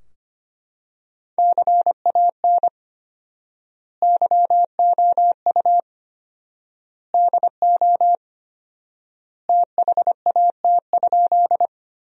Morse code, heard as a keyed tone, about 25 wpm, 700 Hz. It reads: CAN YOU DO THAT?
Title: 4x Word-Spacing - Sentences from Top 100 Words - 25wpm